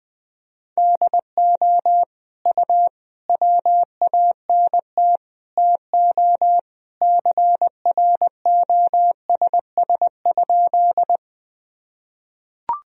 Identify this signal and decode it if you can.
Morse code, heard as a keyed tone, about 20 wpm, 700 Hz. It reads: DO U WANT TO CROSS?
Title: Sentences from Top 300 Words and Abbreviations 20wpm